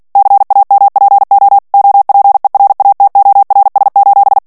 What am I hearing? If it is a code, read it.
Morse code, heard as a keyed tone, about 48 wpm, 800 Hz. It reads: CAMPO OPERATORS:
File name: ITALIEN texte clair 2 vitesse 45 mots minute